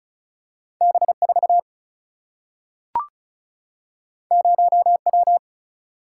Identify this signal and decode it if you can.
Morse code, heard as a keyed tone, about 36 wpm, 700 Hz. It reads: B 4 0 W
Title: Single-Letter Single-Number 35wpm